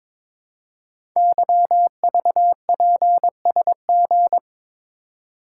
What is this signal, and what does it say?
Morse code, heard as a keyed tone, about 22 wpm, 700 Hz. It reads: YVPSG
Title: Code Groups - 5 Letters - 22wpm